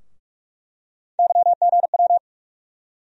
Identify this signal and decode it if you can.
Morse code, heard as a keyed tone, about 44 wpm, 700 Hz. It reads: YGW
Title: USA Call Sign Postfixes - 3 Letters 45wpm